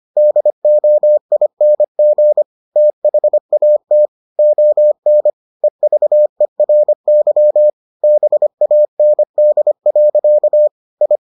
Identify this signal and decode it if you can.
Morse code, heard as a keyed tone, about 25 wpm, 600 Hz. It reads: DOING THAT ON EVERY BAND. I